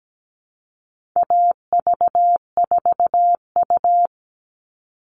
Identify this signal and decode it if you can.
Morse code, heard as a keyed tone, about 17 wpm, 700 Hz. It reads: AV4U
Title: All Cut Numbers - 4-Digit 17wpm